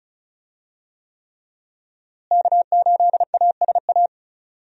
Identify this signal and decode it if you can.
Morse code, heard as a keyed tone, about 34 wpm, 700 Hz. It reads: K8ASA
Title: USA Call Signs - 5 Characters - Repeated 3x - 35wpm